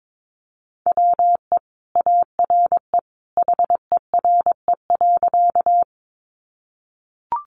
Morse code, heard as a keyed tone, about 22 wpm, 700 Hz. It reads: WE ARE HERE.
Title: Sentences from Top 200 Words 22wpm